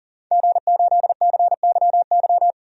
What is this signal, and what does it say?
Morse code, heard as a keyed tone, about 40 wpm, 700 Hz. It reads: G8CYY
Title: DXCC-100 Call Signs - 5 Characters - 40wpm